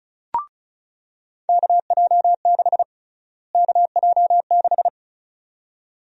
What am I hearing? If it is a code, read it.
Morse code, heard as a keyed tone, about 36 wpm, 700 Hz. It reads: KJ6 KJ6
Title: USA Call Sign Prefixes - Repeated 2x - 35wpm